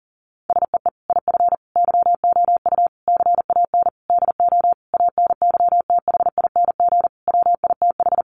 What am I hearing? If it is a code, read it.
Morse code, heard as a keyed tone, about 40 wpm, 700 Hz. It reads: SEE IF YOU CAN DO ANYTHING WITH
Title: Sentences from Top 1000 Words 40wpm